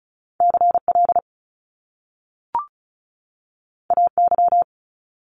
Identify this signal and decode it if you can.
Morse code, heard as a keyed tone, about 36 wpm, 700 Hz. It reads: CL AY
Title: Top Two Letter Combinations in Words 35wpm